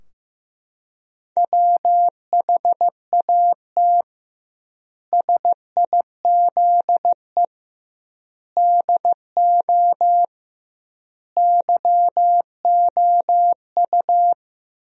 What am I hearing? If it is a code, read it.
Morse code, heard as a keyed tone, about 15 wpm, 700 Hz. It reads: WHAT SIZE DO YOU
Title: 2x Word-Spacing - Sentences from Top 400 Words - 15wpm